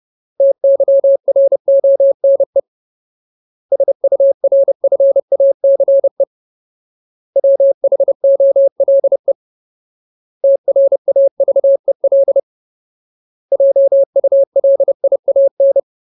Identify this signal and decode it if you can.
Morse code, heard as a keyed tone, about 30 wpm, 550 Hz. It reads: TYRONE SURFACE WHOLE TRAVEL JULIAN
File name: Ultimate Word List / 30WPM Ultimate